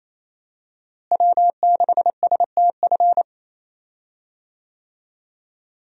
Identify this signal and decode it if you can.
Morse code, heard as a keyed tone, about 28 wpm, 700 Hz. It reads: W6STF
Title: USA Call Signs - 5 Characters - Repeated 2x - 28wpm